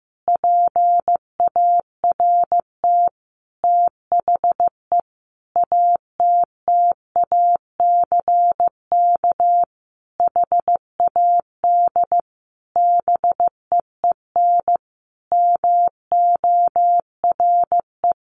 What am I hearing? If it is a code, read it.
Morse code, heard as a keyed tone, about 15 wpm, 700 Hz. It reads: PART THE ATTACK HAD BEEN MORE
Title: Sentences from Top 1500 Words 15wpm